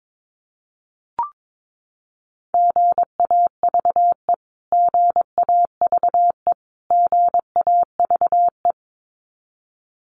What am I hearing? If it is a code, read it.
Morse code, heard as a keyed tone, about 22 wpm, 700 Hz. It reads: GAVE GAVE GAVE
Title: Top 500 Words Repeated 22wpm